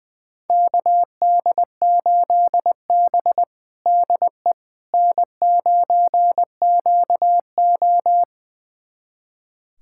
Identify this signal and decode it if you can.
Morse code, heard as a keyed tone, about 20 wpm, 700 Hz. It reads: KD8B DE N9QO
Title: USA Callsign DE Callsign - 4 Characters - 20wpm